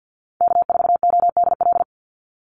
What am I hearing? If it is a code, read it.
Morse code, heard as a keyed tone, about 50 wpm, 700 Hz. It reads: K4ODL
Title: Call Signs - 50wpm